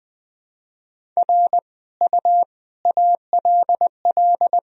Morse code, heard as a keyed tone, about 21 wpm, 700 Hz. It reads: R U ALL
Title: Sentences from Top 400 Words and Abbreviations 20wpm